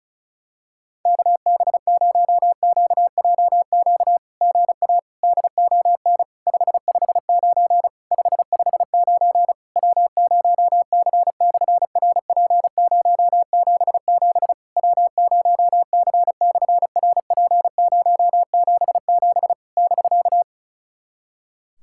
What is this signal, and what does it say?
Morse code, heard as a keyed tone, about 36 wpm, 700 Hz. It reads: KB0QJQ GA DON 559 559 W0C/RP077 W0C/RP077 <BK>
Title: SOTA QSO Practice 35wpm